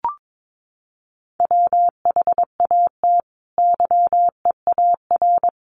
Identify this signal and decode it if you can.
Morse code, heard as a keyed tone, about 22 wpm, 700 Hz. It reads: WHAT YEAR
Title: Sentences from Top 800 Words 22wpm